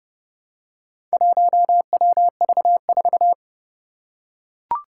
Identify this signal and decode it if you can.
Morse code, heard as a keyed tone, about 30 wpm, 700 Hz. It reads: 1WV4
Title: Four Challenging Letter-Number Combinations 30wpm